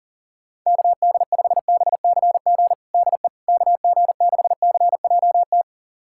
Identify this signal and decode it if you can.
Morse code, heard as a keyed tone, about 40 wpm, 700 Hz. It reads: KD5BCG DE KG6CJT
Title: USA Callsign DE Callsign 40wpm